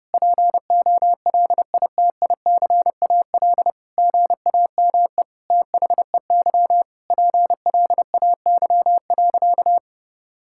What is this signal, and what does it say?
Morse code, heard as a keyed tone, about 32 wpm, 700 Hz. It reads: POLITICAL GAME THEY PLAY.
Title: Sentences from Top 1000 Words - Review Part 5 - 30wpm